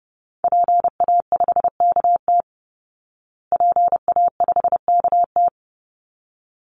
Morse code, heard as a keyed tone, about 30 wpm, 700 Hz. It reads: PA5KT PA5KT
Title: Call Signs Repeated 3x - 30wpm